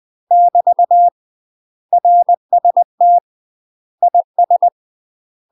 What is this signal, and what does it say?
Morse code, heard as a keyed tone, about 20 wpm, 700 Hz. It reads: <BT> RST IS
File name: UK MASTER QSOs TRAINING/20WPM